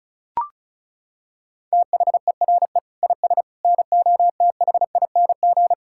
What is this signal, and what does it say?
Morse code, heard as a keyed tone, about 34 wpm, 700 Hz. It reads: THERE IS NOTHING
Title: Sentences from Top 1000 Words - Review Part 5 - 35wpm